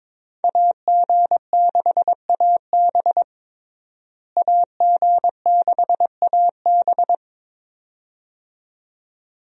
Morse code, heard as a keyed tone, about 22 wpm, 700 Hz. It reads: AG6AB AG6AB
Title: USA Call Signs - 5 Characters - Repeated 2x - 22wpm